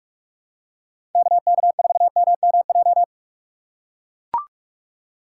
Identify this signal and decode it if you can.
Morse code, heard as a keyed tone, about 43 wpm, 700 Hz. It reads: KK4MMJ
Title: USA Call Signs - 6 Characters - 6x Character Spacing - 45wpm